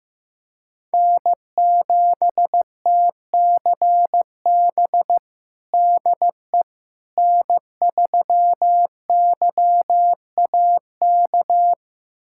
Morse code, heard as a keyed tone, about 15 wpm, 700 Hz. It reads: N7TCB DE N3YAK
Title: USA Callsign DE Callsign - 5 Characters - 15wpm